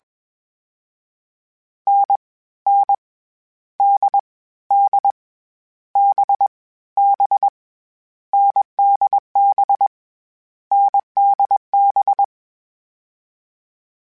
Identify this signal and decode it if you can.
Morse code, heard as a keyed tone, about 21 wpm, 800 Hz. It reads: N N D D B B NDB NDB
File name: Difficult Combos / Difficult Combos-mp3s / awj1 ndb6 gz7 @20